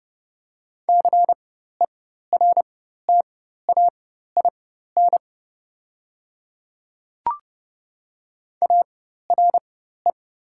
Morse code, heard as a keyed tone, about 29 wpm, spaced as for 14 wpm, 690 Hz. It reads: CERTAIN ARE
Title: 4x Character Spacing - Top 500 Words - 30wpm